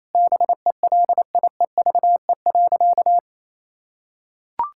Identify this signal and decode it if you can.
Morse code, heard as a keyed tone, about 28 wpm, 700 Hz. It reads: BELIEVE.
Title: Sentences from Top 1000 Words - Review Part 6 - 28wpm